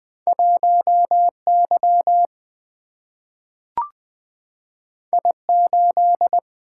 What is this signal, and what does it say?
Morse code, heard as a keyed tone, about 20 wpm, 700 Hz. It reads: 1Y I8
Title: Single-Letter Single-Number 20wpm